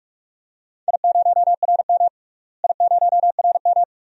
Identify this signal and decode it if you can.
Morse code, heard as a keyed tone, about 41 wpm, 700 Hz. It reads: I0RM I0RM
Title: DXCC-100 Call Signs - 4 Characters - Repeated 2x - 45wpm